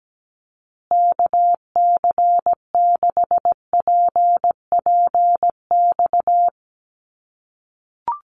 Morse code, heard as a keyed tone, about 17 wpm, 700 Hz. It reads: KC6PPX
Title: USA Call Signs Repeated 3x 17wpm